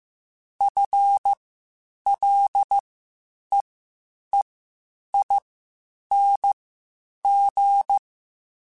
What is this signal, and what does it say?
Morse code, heard as a keyed tone, about 15 wpm, spaced as for 8 wpm, 800 Hz. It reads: FLEEING
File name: Spaced / Chuckles Spaced-mp3s / Temptation sp @13